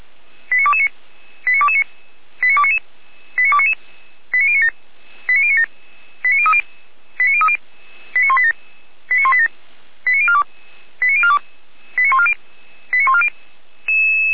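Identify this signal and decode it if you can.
Five-tone selective call (ZVEI), heard as quick runs of five tones, tones of about 70 ms. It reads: 89229 89229 89299 89299 89097 89097 89030 89030 89187 89187 89042 89042 89140 89140
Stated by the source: ZVEI five-tone selective call received on an SDR, fire brigade Feuerwehr Orlamünde.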